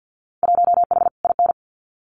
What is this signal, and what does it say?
Morse code, heard as a keyed tone, about 48 wpm, 700 Hz. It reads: 1H IN
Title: Winter Field Day Exchanges 50wpm